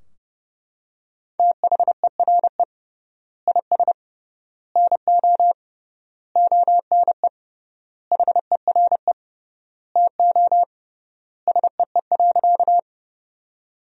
Morse code, heard as a keyed tone, about 30 wpm, 700 Hz. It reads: THERE IS NO ONE HERE TO SEE.
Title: 3x Word-Spacing - Sentences from Top 200 Words - 30wpm